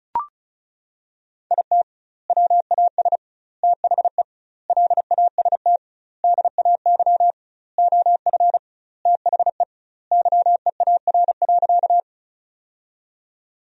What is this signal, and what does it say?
Morse code, heard as a keyed tone, about 36 wpm, 700 Hz. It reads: IT WAS THE LAST DAY OF THE YEAR.
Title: 2x Word-Spacing - Sentences from Top 300 Words - 35wpm